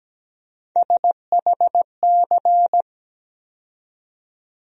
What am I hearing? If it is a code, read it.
Morse code, heard as a keyed tone, about 17 wpm, 700 Hz. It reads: SHC
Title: USA Call Sign Postfixes - 3 Letters 17wpm